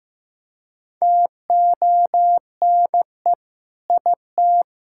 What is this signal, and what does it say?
Morse code, heard as a keyed tone, about 15 wpm, 700 Hz. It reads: TONE IT
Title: Sentences from Top 1600 Words 15wpm